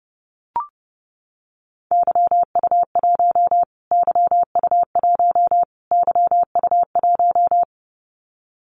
Morse code, heard as a keyed tone, about 30 wpm, 700 Hz. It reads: YU1 YU1 YU1
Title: Callsign Prefixes for Top 25 DXCC - Repeated 3x - 30wpm